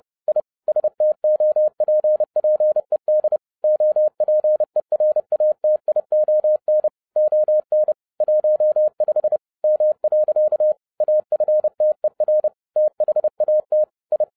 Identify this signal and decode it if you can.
Morse code, heard as a keyed tone, about 30 wpm, 610 Hz. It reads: I STOPPED OPERATION ON 15 M. AFTER THAT I